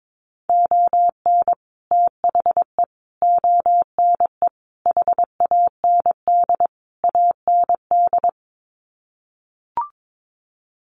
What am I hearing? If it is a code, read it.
Morse code, heard as a keyed tone, about 22 wpm, 700 Hz. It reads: ON THE ONE HAND AND
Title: Sets of 5 Words Encore2 22wpm